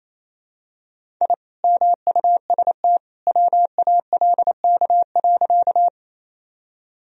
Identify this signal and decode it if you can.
Morse code, heard as a keyed tone, about 28 wpm, 700 Hz. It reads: I MUST WALK.
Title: Sentences from Top 1000 Words - Review Part 1 - 28wpm